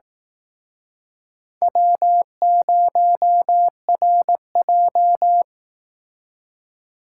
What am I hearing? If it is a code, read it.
Morse code, heard as a keyed tone, about 19 wpm, 700 Hz. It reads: W0RJ
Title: Speed Racing - USA 1x2 Call Signs 15wpm